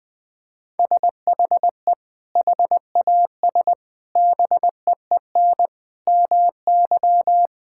Morse code, heard as a keyed tone, about 20 wpm, 700 Hz. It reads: SHE HAS BEEN MY
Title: Sentences with 6-Word Phrases - 20wpm